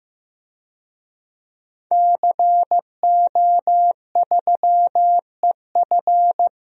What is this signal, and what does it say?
Morse code, heard as a keyed tone, about 15 wpm, 700 Hz. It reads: CO3EF
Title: DXCC-100 Call Signs - 5 Characters - 15wpm